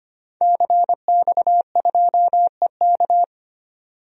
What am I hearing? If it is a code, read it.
Morse code, heard as a keyed tone, about 25 wpm, 700 Hz. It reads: CX2EK
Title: DXCC-100 Call Signs - 5 Characters - 25wpm